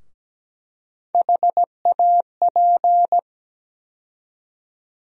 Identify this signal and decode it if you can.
Morse code, heard as a keyed tone, about 17 wpm, 700 Hz. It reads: HAP
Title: USA Call Sign Postfixes - 3 Letters 17wpm